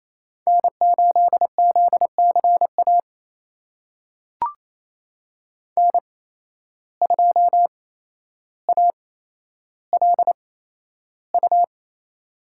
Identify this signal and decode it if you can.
Morse code, heard as a keyed tone, about 28 wpm, spaced as for 12 wpm, 700 Hz. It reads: N8ZCA N2ALU
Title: USA Call Signs - 5 Characters - 8x Character Spacing - 28wpm